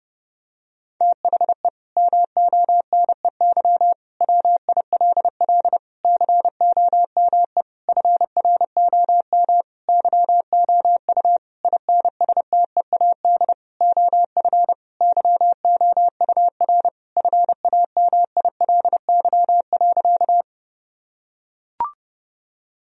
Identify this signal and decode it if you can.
Morse code, heard as a keyed tone, about 32 wpm, 700 Hz. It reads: THE MONEY WILL COME FROM YOU INSTEAD OF YOUR FAMILY.
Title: Sentences from Top 1000 Words - Review Part 4 - 30wpm